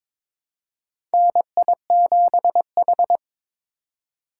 Morse code, heard as a keyed tone, about 22 wpm, 700 Hz. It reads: NI7H
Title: USA Call Signs - 8x Character Spacing - 22wpm